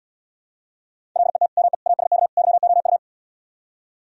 Morse code, heard as a keyed tone, about 78 wpm, 700 Hz. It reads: LET ME ASK YOU